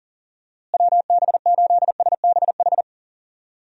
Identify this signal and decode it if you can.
Morse code, heard as a keyed tone, about 40 wpm, 700 Hz. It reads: WB8SBH
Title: USA Call Signs - 6 Characters - Repeated 3x - 40wpm